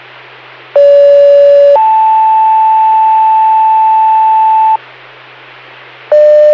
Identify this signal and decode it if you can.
Two-tone page, heard as a one-second tone, then a three-second tone, about 570 then 880 Hz. It is heard from 0.8 to 4.8 s.